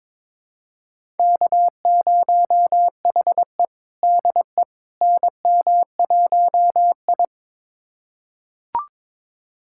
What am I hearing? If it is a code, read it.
Morse code, heard as a keyed tone, about 22 wpm, 700 Hz. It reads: K0HE DE NM1I
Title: USA Callsign DE Callsign - 4 Characters - 22wpm